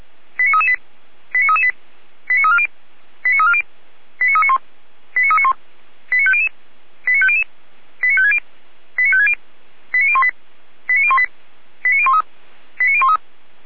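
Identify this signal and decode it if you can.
Five-tone selective call (ZVEI), heard as quick runs of five tones, tones of about 70 ms. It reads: 89338 89338 89340 89340 89391 89391 89500 89500 89560 89560 89018 89018 89013 89013
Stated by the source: ZVEI five-tone call 89315 received on an SDR.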